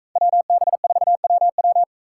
Morse code, heard as a keyed tone, about 44 wpm, 700 Hz. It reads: WB4WW